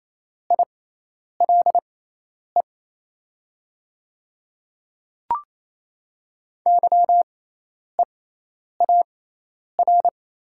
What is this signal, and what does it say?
Morse code, heard as a keyed tone, about 28 wpm, spaced as for 10 wpm, 700 Hz. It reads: ILE YEAR
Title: 6x Character Spacing - Top 300 Words - 28wpm